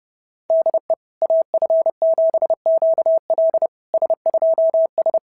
Morse code, heard as a keyed tone, about 30 wpm, 690 Hz. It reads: DE AF7QL S2S
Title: SOTA QSO Practice 30wpm